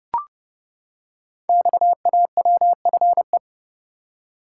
Morse code, heard as a keyed tone, about 29 wpm, 700 Hz. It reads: XAWFE
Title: Code Groups - 5 Letters - 30wpm